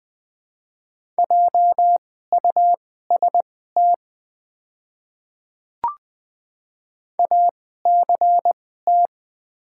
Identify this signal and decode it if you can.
Morse code, heard as a keyed tone, about 20 wpm, spaced as for 14 wpm, 700 Hz. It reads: JUST ACT
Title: 2x Character Spacing - Top 200 Words - 20wpm